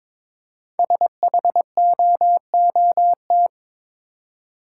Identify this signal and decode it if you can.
Morse code, heard as a keyed tone, about 22 wpm, 700 Hz. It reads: SHOOT
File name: practice / 22wpm / 5 Letter Words - First 3 Letters Given - 22wpm